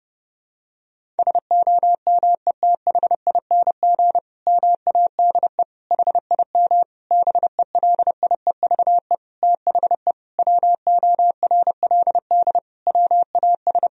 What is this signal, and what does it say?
Morse code, heard as a keyed tone, about 30 wpm, 700 Hz. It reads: SOMETHING MADE HIM BELIEVE THE WORLD WAS
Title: Sentences from Top 1000 Words - Review Part 1 - 30wpm